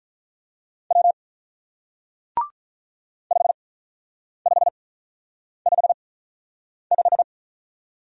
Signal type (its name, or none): Morse code